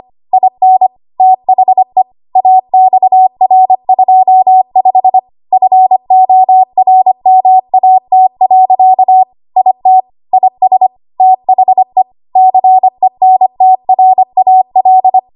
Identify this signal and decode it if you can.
Morse code, heard as a keyed tone, about 25 wpm, 750 Hz. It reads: IN THE AXR25 FORMAT. IT IS THE CENTRAL
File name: Morse / 25WPM